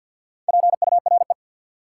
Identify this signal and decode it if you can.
Morse code, heard as a keyed tone, about 49 wpm, 700 Hz. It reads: PURE
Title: Sentences from Top 1500 Words - New Word Drill - 50wpm